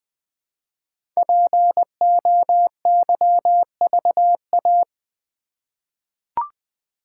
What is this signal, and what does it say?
Morse code, heard as a keyed tone, about 20 wpm, 700 Hz. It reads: POYVA
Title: Code Groups - 5 Letters - 20wpm